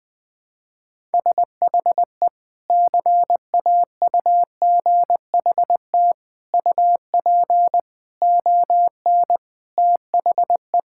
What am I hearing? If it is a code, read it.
Morse code, heard as a keyed tone, about 20 wpm, 700 Hz. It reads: SHE CAUGHT UP ON THE
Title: Sentences with 3-Word Phrases - 20wpm